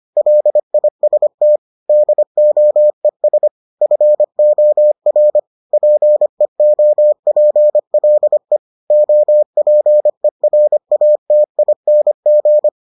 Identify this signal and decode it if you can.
Morse code, heard as a keyed tone, about 25 wpm, 600 Hz. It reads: LIST DOES FOR PEOPLE OPERATING